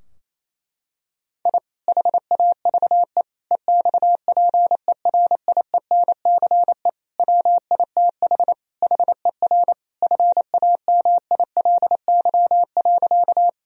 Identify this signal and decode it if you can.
Morse code, heard as a keyed tone, about 27 wpm, 700 Hz. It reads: I HAVE EXPERIENCE WITH HER FAMILY.
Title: Sentences from Top 1000 Words - Review Part 3 - 28wpm